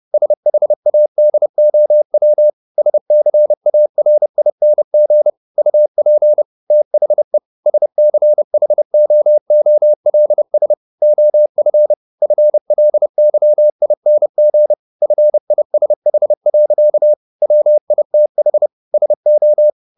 Morse code, heard as a keyed tone, about 30 wpm, 600 Hz. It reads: SHADOW SCARING UP THE SCHOOLS OF FLYING FISH. WITH SO
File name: the old man and the sea 2 30wpm